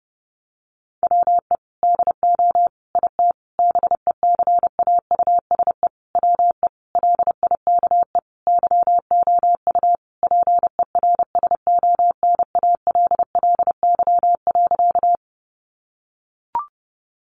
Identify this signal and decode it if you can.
Morse code, heard as a keyed tone, about 30 wpm, 700 Hz. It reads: WE DO IT BECAUSE WE LIKE YOU PERSONALLY.
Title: Sentences from Top 1400 Words 30wpm